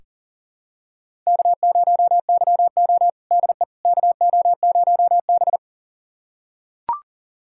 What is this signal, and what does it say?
Morse code, heard as a keyed tone, about 40 wpm, 700 Hz. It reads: K0YO DE KO0B
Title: USA Callsign DE Callsign - 4 Characters - 40wpm